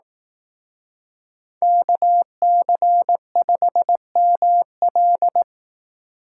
Morse code, heard as a keyed tone, about 18 wpm, 700 Hz. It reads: KC5ML
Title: Speed Racing - USA Call Signs 15wpm